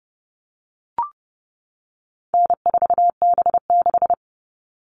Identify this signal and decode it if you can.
Morse code, heard as a keyed tone, about 30 wpm, 700 Hz. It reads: N4B6
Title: All Cut Numbers - 4-Digit 30wpm